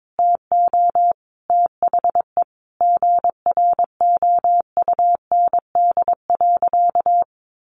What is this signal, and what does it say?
Morse code, heard as a keyed tone, about 22 wpm, 700 Hz. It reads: TO THE GROUND.
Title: Sentences from Top 500 Words 22wpm